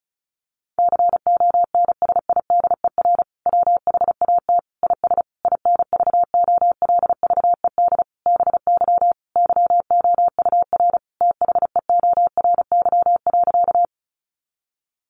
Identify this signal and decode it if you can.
Morse code, heard as a keyed tone, about 34 wpm, 700 Hz. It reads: CONSIDER WHAT IS INVOLVED BY YOUR THEORY.
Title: Sentences from Top 800 Words 35wpm